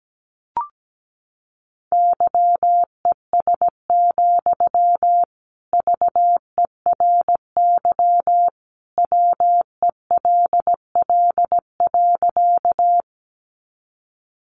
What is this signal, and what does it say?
Morse code, heard as a keyed tone, about 17 wpm, 700 Hz. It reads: YES, VERY WELL.
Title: Sentences from Top 500 Words 17wpm